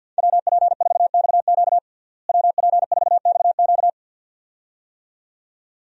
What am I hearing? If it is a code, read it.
Morse code, heard as a keyed tone, about 50 wpm, 700 Hz. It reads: WP4XQ WP4XQ
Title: USA Call Signs - 5 Characters - Repeated 2x - 50wpm